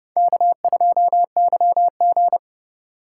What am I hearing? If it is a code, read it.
Morse code, heard as a keyed tone, about 30 wpm, 700 Hz. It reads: K2YG
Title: USA Call Signs - 2x Character Spacing - 30wpm